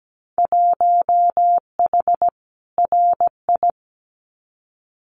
Morse code, heard as a keyed tone, about 17 wpm, 700 Hz. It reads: 1H RI
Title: Winter Field Day Exchanges 17wpm